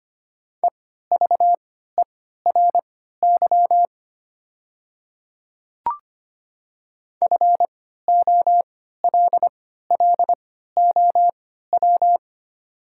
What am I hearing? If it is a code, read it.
Morse code, heard as a keyed tone, about 25 wpm, spaced as for 14 wpm, 700 Hz. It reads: EVERY FOLLOW
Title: 3x Character Spacing - Top 200 Words - 25wpm